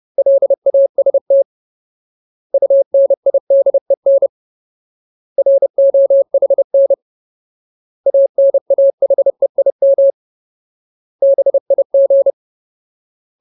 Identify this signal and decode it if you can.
Morse code, heard as a keyed tone, about 30 wpm, 550 Hz. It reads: LAST UNIDEN ROHN ANAHEIM BIG
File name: Ultimate Word List / 30WPM Ultimate